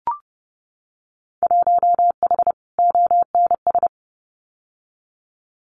Morse code, heard as a keyed tone, about 31 wpm, 700 Hz. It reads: 1H ONS
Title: Winter Field Day Exchanges 30wpm